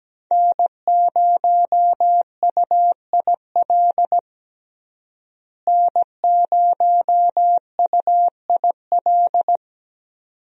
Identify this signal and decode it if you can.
Morse code, heard as a keyed tone, about 17 wpm, 700 Hz. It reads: N0UIL N0UIL
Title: USA Call Signs - 5 Characters - Repeated 2x - 17wpm